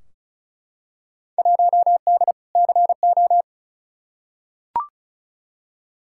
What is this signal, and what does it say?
Morse code, heard as a keyed tone, about 36 wpm, 700 Hz. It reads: 1D CO
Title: Field Day Exchanges - 35wpm